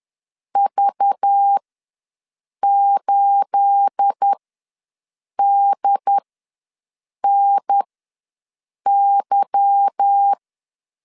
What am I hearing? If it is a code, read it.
Morse code, heard as a keyed tone, about 10 wpm, spaced as for 6 wpm, 800 Hz. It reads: V8DNY
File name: LEZIONI / 9 28 ws 10wpm 4124